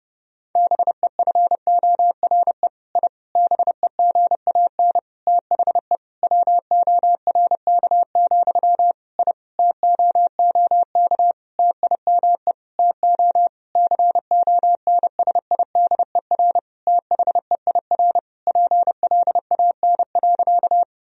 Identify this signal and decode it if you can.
Morse code, heard as a keyed tone, about 30 wpm, 700 Hz. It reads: BEFORE I BEGAN THE WORK, I TOOK TIME TO CONSIDER THEIR PLAN.
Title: Sentences from Top 1000 Words - Review Part 4 - 30wpm